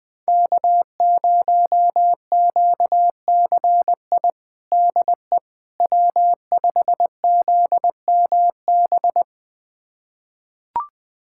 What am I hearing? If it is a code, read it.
Morse code, heard as a keyed tone, about 20 wpm, 700 Hz. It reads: K0QCI DE W5ZMB
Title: USA Callsign DE Callsign - 5 Characters - 20wpm